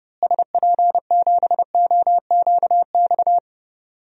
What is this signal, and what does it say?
Morse code, heard as a keyed tone, about 30 wpm, 700 Hz. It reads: SP7OQX
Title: DXCC-100 Call Signs - 6 Characters - Repeated 2x - 30wpm